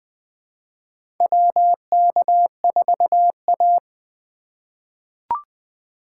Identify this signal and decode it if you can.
Morse code, heard as a keyed tone, about 20 wpm, 700 Hz. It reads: WK4A
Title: USA Call Signs - 4 Characters - 3x Character Spacing - 20wpm